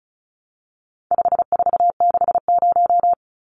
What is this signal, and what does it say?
Morse code, heard as a keyed tone, about 34 wpm, 700 Hz. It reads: 5460
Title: Numbers - 4-Digit 35wpm